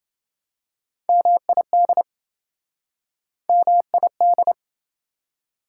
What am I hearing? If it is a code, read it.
Morse code, heard as a keyed tone, about 28 wpm, 700 Hz. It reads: MID MID